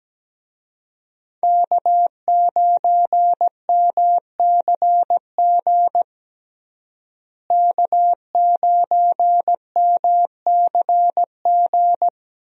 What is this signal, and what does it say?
Morse code, heard as a keyed tone, about 17 wpm, 700 Hz. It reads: K9MCG K9MCG
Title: USA Call Signs - 5 Characters - Repeated 2x - 17wpm